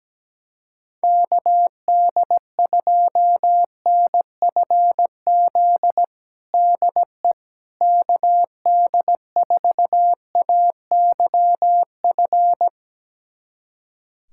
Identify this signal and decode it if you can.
Morse code, heard as a keyed tone, about 17 wpm, 700 Hz. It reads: KD2NFZ DE KD4AYF
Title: USA Callsign DE Callsign - 6 Characters - 17wpm